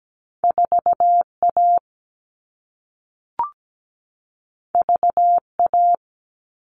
Morse code, heard as a keyed tone, about 17 wpm, 700 Hz. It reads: 4A VA
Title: All Cut Numbers - 2-Digit 17wpm